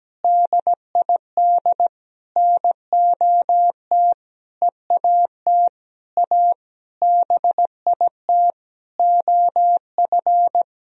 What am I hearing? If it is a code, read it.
Morse code, heard as a keyed tone, about 17 wpm, 700 Hz. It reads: DID NOT EAT A BIT OF
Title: Sentences from Top 1000 Words - Review Part 3 - 17wpm